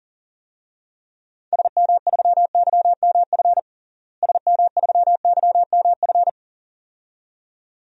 Morse code, heard as a keyed tone, about 40 wpm, 700 Hz. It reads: SM3YMF SM3YMF
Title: DXCC-100 Call Signs - 6 Characters - Repeated 2x - 40wpm